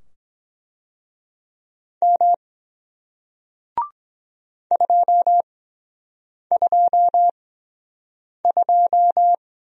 Morse code, heard as a keyed tone, about 23 wpm, spaced as for 7 wpm, 700 Hz. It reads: M 222